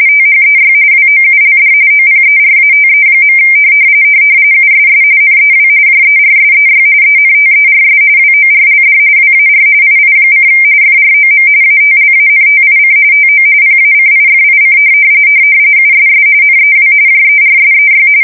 RTTY, about 45.45 baud. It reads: THEY WOULD OPERATE. USS NEW JERSEY ENTERS WORLD WAR II: FROM ATLANTIC TO PACIFIC USS NEW JERSEY
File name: fsk170 / battleshipnj